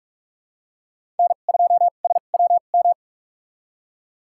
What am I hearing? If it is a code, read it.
Morse code, heard as a keyed tone, about 46 wpm, spaced as for 32 wpm, 700 Hz. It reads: N2SWM
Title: USA Call Signs - 5 Characters - 2x Character Spacing - 45wpm